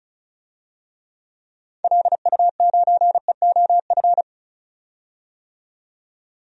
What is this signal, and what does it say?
Morse code, heard as a keyed tone, about 34 wpm, 700 Hz. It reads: LU9EOF
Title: DXCC-100 Call Signs - 6 Characters - 35wpm